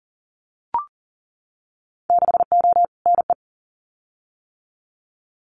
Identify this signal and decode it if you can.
Morse code, heard as a keyed tone, about 40 wpm, 700 Hz. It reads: 6O NE
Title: Winter Field Day Exchanges Encore 40wpm